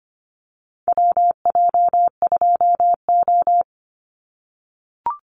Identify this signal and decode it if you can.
Morse code, heard as a keyed tone, about 25 wpm, 700 Hz. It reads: WJ2O
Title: Call Signs - 25wpm